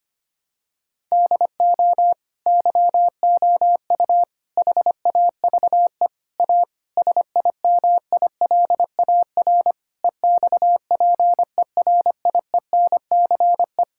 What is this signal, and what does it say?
Morse code, heard as a keyed tone, about 25 wpm, 700 Hz. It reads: DO YOU HAVE A SIMILAR EXPERIENCE
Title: Sentences from Top 1000 Words - Review Part 5 - 25wpm